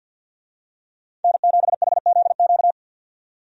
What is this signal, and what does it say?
Morse code, heard as a keyed tone, about 51 wpm, 700 Hz. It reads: N7HZQ
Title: USA Call Signs - 5 Characters - 50wpm